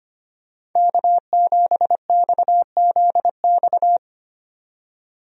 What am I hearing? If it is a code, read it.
Morse code, heard as a keyed tone, about 25 wpm, 700 Hz. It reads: K7XZX
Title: USA Call Signs - 5 Characters - 25wpm